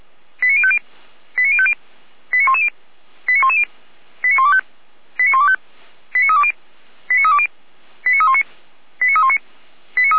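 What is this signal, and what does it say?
Five-tone selective call (ZVEI), heard as quick runs of five tones, tones of about 70 ms. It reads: 89050 89050 89110 89110 89125 89125 89320 89320 89319 89319
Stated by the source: ZVEI five-tone selective call received on an SDR.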